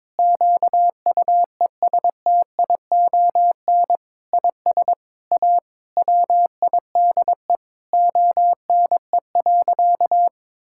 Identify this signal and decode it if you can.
Morse code, heard as a keyed tone, about 22 wpm, 700 Hz. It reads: QUESTION IS A WIDE ONE.
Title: Sentences from Top 1000 Words - Review Part 6 - 22wpm